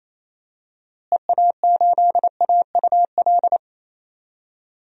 Morse code, heard as a keyed tone, about 28 wpm, 700 Hz. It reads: EA8AUL
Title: DXCC-100 Call Signs - 6 Characters - Repeated 3x - 28wpm